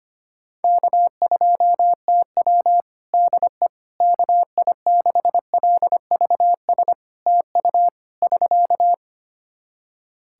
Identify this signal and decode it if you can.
Morse code, heard as a keyed tone, about 25 wpm, 700 Hz. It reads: K2TW DE KI6LVS TU <SK>
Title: Practice QSO - Exchange 4 - Ending - 25wpm